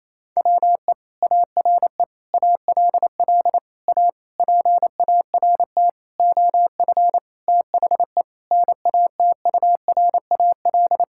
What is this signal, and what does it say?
Morse code, heard as a keyed tone, about 27 wpm, 700 Hz. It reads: WE ARE ALL A PART OF THE NATURAL
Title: Sentences from Top 1000 Words - Review Part 6 - 28wpm